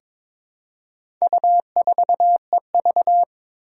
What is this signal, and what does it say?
Morse code, heard as a keyed tone, about 22 wpm, 700 Hz. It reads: U4EV
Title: Four Challenging Letter-Number Combinations 22wpm